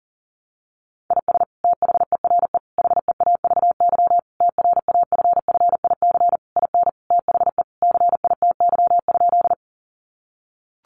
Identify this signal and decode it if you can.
Morse code, heard as a keyed tone, about 40 wpm, 700 Hz. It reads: IS THERE HEAVY TRAFFIC IN THE CITY?
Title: Sentences from Top 1300 Words 40wpm